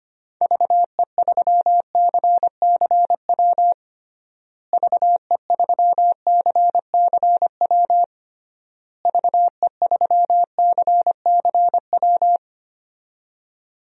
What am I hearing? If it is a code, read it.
Morse code, heard as a keyed tone, about 25 wpm, 700 Hz. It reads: VE3CCW VE3CCW VE3CCW
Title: DXCC-100 Call Signs - 6 Characters - Repeated 3x - 25wpm